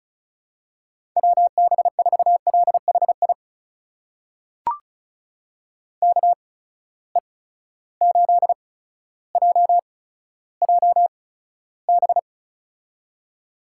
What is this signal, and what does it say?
Morse code, heard as a keyed tone, about 36 wpm, spaced as for 15 wpm, 700 Hz. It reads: WB4LHI KE8JJB
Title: USA Call Signs - 6 Characters - 8x Character Spacing - 35wpm